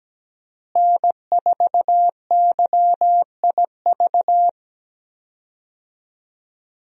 Morse code, heard as a keyed tone, about 17 wpm, 700 Hz. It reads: N4YIV
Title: USA Call Signs - 5 Characters - Repeated 2x - 17wpm